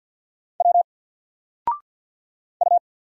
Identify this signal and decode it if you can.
Morse code, heard as a keyed tone, about 50 wpm, 700 Hz. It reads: W U
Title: Single Letter-Number 50wpm